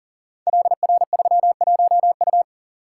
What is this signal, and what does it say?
Morse code, heard as a keyed tone, about 42 wpm, 690 Hz. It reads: LR31U
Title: Code Groups - 5 Letter:Numbers - 40wpm